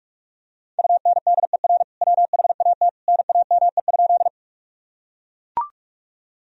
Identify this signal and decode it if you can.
Morse code, heard as a keyed tone, about 42 wpm, 700 Hz. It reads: UNDER WHAT NAME?
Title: Sentences from Top 1000 Words - Review Part 1 - 45wpm